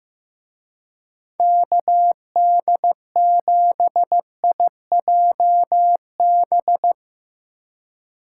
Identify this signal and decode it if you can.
Morse code, heard as a keyed tone, about 17 wpm, 700 Hz. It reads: KD7IJB
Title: Speed Racing - USA 2x3 Call Signs 15wpm